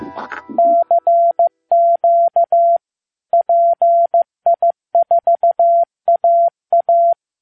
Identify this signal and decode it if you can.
Morse code, heard as a keyed tone, about 15 wpm, 680 Hz. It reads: CQ PI4AA
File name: pi4aa / PI4AA Uitzending